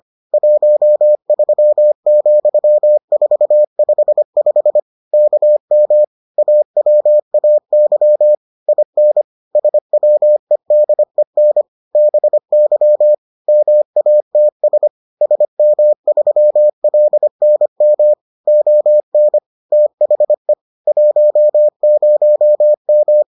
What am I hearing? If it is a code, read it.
Morse code, heard as a keyed tone, about 25 wpm, 600 Hz. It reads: 13,455 KM AWAY IN SWEDEN BY MATS SM3LNM ON THE 10M